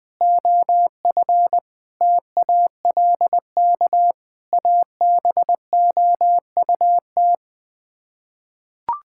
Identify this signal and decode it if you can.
Morse code, heard as a keyed tone, about 20 wpm, 700 Hz. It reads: OF TALK ABOUT
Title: Sets of 8 Words - 20wpm